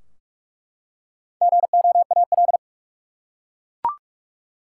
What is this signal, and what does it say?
Morse code, heard as a keyed tone, about 47 wpm, 700 Hz. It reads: GOAL
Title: Four Letter Words - 45wpm